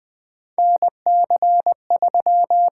Morse code, heard as a keyed tone, about 20 wpm, 700 Hz. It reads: NC3
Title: USA Call Signs - 4 Characters - 8x Character Spacing - 20wpm